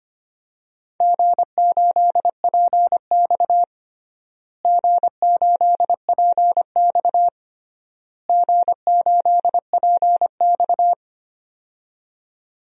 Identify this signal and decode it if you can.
Morse code, heard as a keyed tone, about 25 wpm, 700 Hz. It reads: G8PX G8PX G8PX
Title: DXCC-100 Call Signs - 4 Characters - Repeated 3x - 25wpm